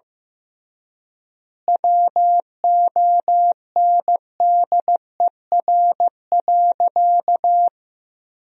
Morse code, heard as a keyed tone, about 15 wpm, 700 Hz. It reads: WONDER.
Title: 3x Word-Spacing - Sentences from Top 500 Words - 15wpm